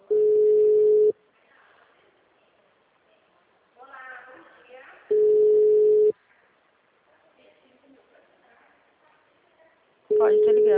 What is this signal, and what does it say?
Ringback tone (Europe, 425 Hz), with speech.